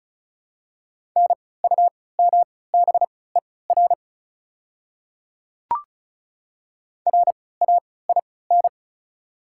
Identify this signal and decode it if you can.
Morse code, heard as a keyed tone, about 36 wpm, spaced as for 20 wpm, 690 Hz. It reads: NUMBER RAIN
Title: 3x Character Spacing - Top 300 Words - 35wpm